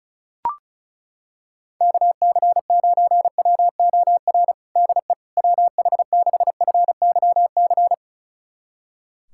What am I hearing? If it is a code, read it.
Morse code, heard as a keyed tone, about 36 wpm, 700 Hz. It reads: KC9WOR DE WH6FYC
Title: USA Callsign DE Callsign 35wpm